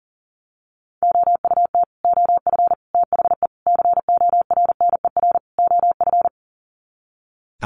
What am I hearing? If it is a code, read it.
Morse code, heard as a keyed tone, about 40 wpm, 700 Hz. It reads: OUT OF THE CORNER OF
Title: Sets of 5 Words 40wpm